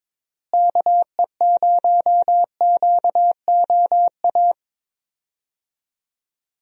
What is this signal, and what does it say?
Morse code, heard as a keyed tone, about 22 wpm, 700 Hz. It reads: KE0QOA
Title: USA Call Signs - 6 Characters - Repeated 3x - 22wpm